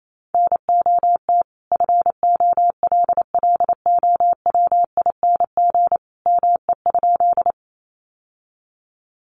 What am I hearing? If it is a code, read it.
Morse code, heard as a keyed tone, about 28 wpm, 700 Hz. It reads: NOT FOLLOWING ME?
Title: Sentences from Top 700 Words 28wpm